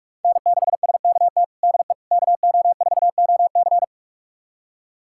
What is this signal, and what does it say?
Morse code, heard as a keyed tone, about 46 wpm, 700 Hz. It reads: N6SKT DE KO4OC
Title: USA Callsign DE Callsign - 5 Characters - 45wpm